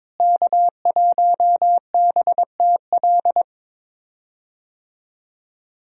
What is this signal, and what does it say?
Morse code, heard as a keyed tone, about 21 wpm, 700 Hz. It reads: K1BTL
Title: USA Call Signs - 5 Characters - Repeated 2x - 22wpm